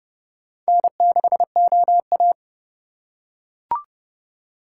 Morse code, heard as a keyed tone, about 27 wpm, 700 Hz. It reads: N6OA